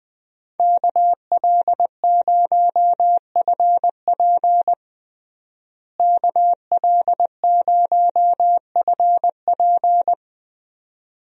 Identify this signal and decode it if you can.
Morse code, heard as a keyed tone, about 20 wpm, 700 Hz. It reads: KL0FP KL0FP
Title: USA Call Signs - 5 Characters - Repeated 2x - 20wpm